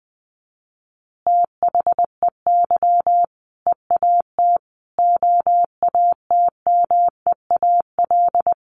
Morse code, heard as a keyed tone, about 20 wpm, 700 Hz. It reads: THEY EAT OATMEAL
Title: Easy Phrases 20wpm